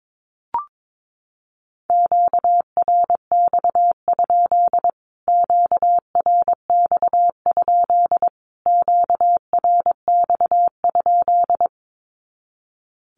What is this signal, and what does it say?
Morse code, heard as a keyed tone, about 22 wpm, 700 Hz. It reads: QRX? QRX? QRX?
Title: Q-Codes Repeated 3x 22wpm